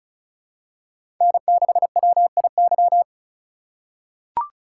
Morse code, heard as a keyed tone, about 34 wpm, 700 Hz. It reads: N6WIY
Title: USA Call Signs - 5 Characters - 3x Character Spacing - 35wpm